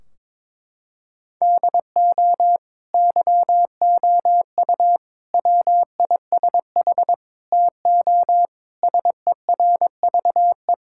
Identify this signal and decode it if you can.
Morse code, heard as a keyed tone, about 22 wpm, 700 Hz. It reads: DO YOU WISH TO SERVE
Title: Sentences from Top 1600 Words 22wpm